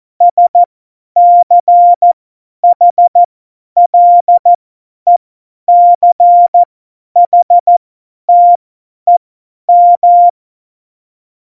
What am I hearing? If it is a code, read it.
Morse code, heard as a keyed tone, about 14 wpm, spaced as for 10 wpm, 700 Hz. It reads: SCHLECHTEM
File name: Morsekurs / Witze / Witz 27 Juni 17, Tempo 50 F70